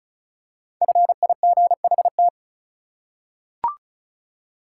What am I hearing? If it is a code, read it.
Morse code, heard as a keyed tone, about 34 wpm, 700 Hz. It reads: FIGHT
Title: Five Letter Words - 35wpm